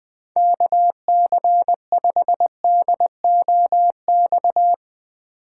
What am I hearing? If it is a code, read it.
Morse code, heard as a keyed tone, about 20 wpm, 700 Hz. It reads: KC5DOX
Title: USA Call Signs - 6 Characters - 2x Character Spacing - 20wpm